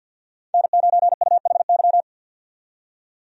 Morse code, heard as a keyed tone, about 53 wpm, 700 Hz. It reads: N9UHY
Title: USA Call Signs - 5 Characters - 50wpm